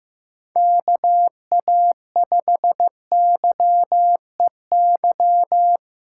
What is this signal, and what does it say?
Morse code, heard as a keyed tone, about 15 wpm, 700 Hz. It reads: KA5YEY
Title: USA Callsign DE Callsign 15wpm